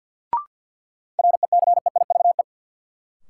Morse code, heard as a keyed tone, about 48 wpm, 700 Hz. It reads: RECEIVE
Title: Sentences from Top 1000 Words - New Word Drill - 50wpm